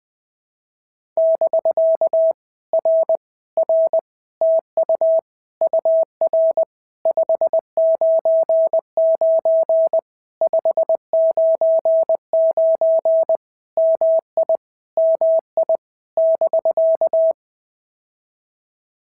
Morse code, heard as a keyed tone, about 20 wpm, 650 Hz. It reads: <BK> R R TU UR 599 599 MI MI <BK>
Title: POTA QSO Practice - 20wpm